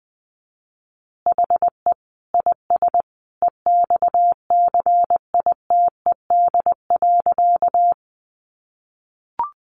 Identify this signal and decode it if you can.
Morse code, heard as a keyed tone, about 20 wpm, 700 Hz. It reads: HE IS EXCITED.
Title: Additional Easy Phrases 20wpm